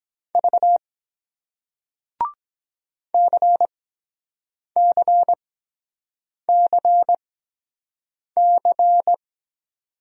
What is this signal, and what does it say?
Morse code, heard as a keyed tone, about 23 wpm, spaced as for 7 wpm, 700 Hz. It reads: V CCCC